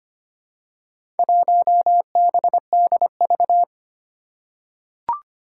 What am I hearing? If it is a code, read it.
Morse code, heard as a keyed tone, about 25 wpm, 700 Hz. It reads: 1BDV
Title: Four Challenging Letter-Number Combinations 25wpm